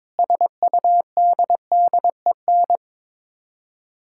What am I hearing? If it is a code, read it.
Morse code, heard as a keyed tone, about 22 wpm, 700 Hz. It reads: SUDDEN
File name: practice / 22wpm / Sets of 6 Words - 22wpm